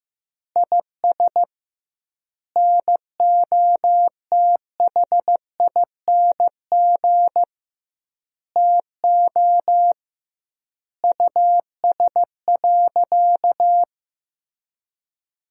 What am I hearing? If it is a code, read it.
Morse code, heard as a keyed tone, about 15 wpm, 700 Hz. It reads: IS NOTHING TO US.
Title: 2x Word-Spacing - Sentences from Top 500 Words - 15wpm